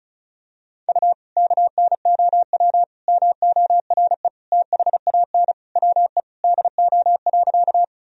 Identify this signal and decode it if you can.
Morse code, heard as a keyed tone, about 36 wpm, 700 Hz. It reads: U KNOW MORE THAN WE DO.
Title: Sentences from Top 100 Words and Abbreviations 35wpm